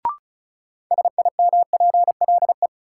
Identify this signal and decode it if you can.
Morse code, heard as a keyed tone, about 34 wpm, 700 Hz. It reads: SIMPLE
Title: Sentences from Top 400 Words - New Word Drill - 35wpm